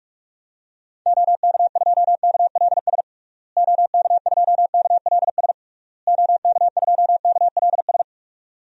Morse code, heard as a keyed tone, about 46 wpm, 700 Hz. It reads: OK2KLS OK2KLS OK2KLS
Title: DXCC-100 Call Signs - 6 Characters - Repeated 3x - 45wpm